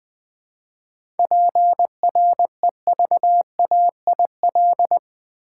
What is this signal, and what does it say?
Morse code, heard as a keyed tone, about 20 wpm, 700 Hz. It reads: PREVAIL
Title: 7 Letter Words - First 3 Letters Given - 20wpm